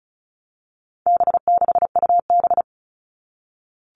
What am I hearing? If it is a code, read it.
Morse code, heard as a keyed tone, about 34 wpm, 700 Hz. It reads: B6UB
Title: All Cut Numbers - 4-Digit 35wpm